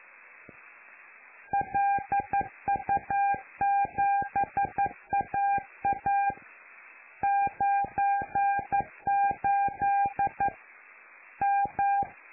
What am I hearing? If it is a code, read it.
Morse code, heard as a keyed tone, about 11 wpm, 800 Hz. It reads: LU7AA 98 M
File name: lu4aao / Globo Amsat